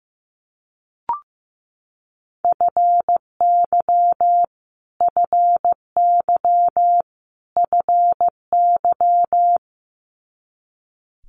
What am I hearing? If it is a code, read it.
Morse code, heard as a keyed tone, about 15 wpm, 700 Hz. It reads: FY FY FY
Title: Multi-letter Phonemes Repeated 3x 15wpm